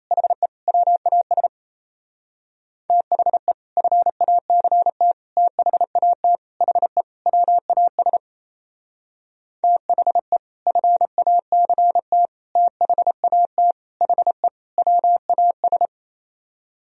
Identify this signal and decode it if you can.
Morse code, heard as a keyed tone, about 35 wpm, 700 Hz. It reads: HE WAS THE FACT THAT HE WAS THE FACT THAT HE WAS
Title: Speed-Racing - Sets of 5 Words 28wpm